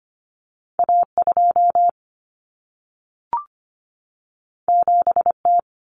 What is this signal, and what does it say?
Morse code, heard as a keyed tone, about 25 wpm, 700 Hz. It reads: A2 7T
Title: Common Cut Numbers - 2-Digit 25wpm